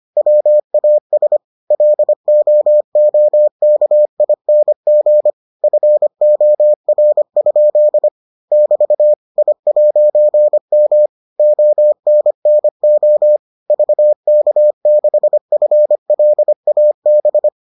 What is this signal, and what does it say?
Morse code, heard as a keyed tone, about 25 wpm, 600 Hz. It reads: WAS LOOKING FOR? <BT> I'M ONNO VK6FLAB